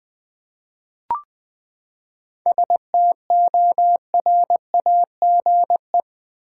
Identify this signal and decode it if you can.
Morse code, heard as a keyed tone, about 21 wpm, 700 Hz. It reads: STORAGE
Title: Seven Letter Words - 20wpm